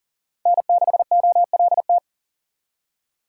Morse code, heard as a keyed tone, about 51 wpm, 700 Hz. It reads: N6OLT